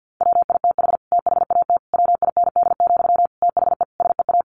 Morse code, heard as a keyed tone, about 48 wpm, 700 Hz. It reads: WITH THAT WIND, THE SEA
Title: Sentences from Top 1400 Words 50wpm